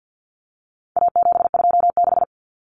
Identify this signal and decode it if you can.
Morse code, heard as a keyed tone, about 50 wpm, 700 Hz. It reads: A726
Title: Common Cut Numbers - 4-Digit 50wpm